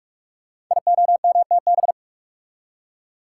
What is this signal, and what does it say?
Morse code, heard as a keyed tone, about 44 wpm, 690 Hz. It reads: IOMTB
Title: Code Groups - 5 Letters - 45wpm